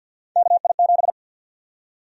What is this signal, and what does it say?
Morse code, heard as a keyed tone, about 52 wpm, 700 Hz. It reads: KI7
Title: USA Call Sign Prefixes - 50wpm